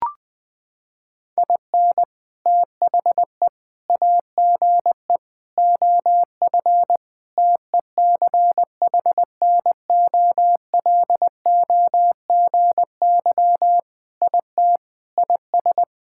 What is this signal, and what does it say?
Morse code, heard as a keyed tone, about 20 wpm, 700 Hz. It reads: IN THE AGE OF TECHNOLOGY IT IS
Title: Sentences with 3-Word Phrases - 20wpm